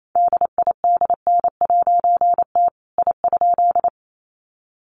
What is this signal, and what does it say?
Morse code, heard as a keyed tone, about 28 wpm, 700 Hz. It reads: DIDN'T I?
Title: A Sound Of Thunder 28wpm